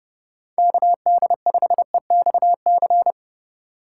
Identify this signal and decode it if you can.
Morse code, heard as a keyed tone, about 30 wpm, 700 Hz. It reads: KD5EXC
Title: USA Call Signs - 6 Characters - 2x Character Spacing - 30wpm